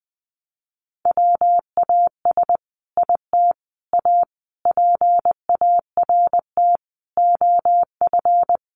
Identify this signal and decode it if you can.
Morse code, heard as a keyed tone, about 20 wpm, 700 Hz. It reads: WAS IT A PART OF
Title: Sentences from Top 1500 Words 20wpm